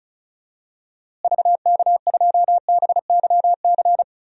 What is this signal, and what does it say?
Morse code, heard as a keyed tone, about 36 wpm, 700 Hz. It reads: VK2BYC
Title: DXCC-100 Call Signs - 6 Characters - Repeated 3x - 35wpm